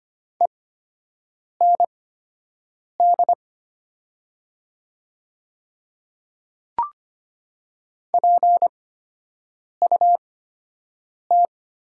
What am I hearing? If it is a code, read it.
Morse code, heard as a keyed tone, about 25 wpm, spaced as for 7 wpm, 700 Hz. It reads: END PUT
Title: 8x Character Spacing - Top 200 Words - 25wpm